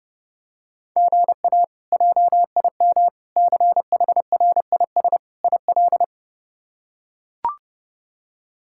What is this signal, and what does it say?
Morse code, heard as a keyed tone, about 30 wpm, 700 Hz. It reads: GA JIM CHRIS IL
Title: K1USN SST QSO Practice - 30wpm